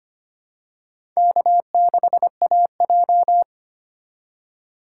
Morse code, heard as a keyed tone, about 25 wpm, 700 Hz. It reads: K6AJ
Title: USA Call Signs - 4 Characters - 4x Character Spacing - 25wpm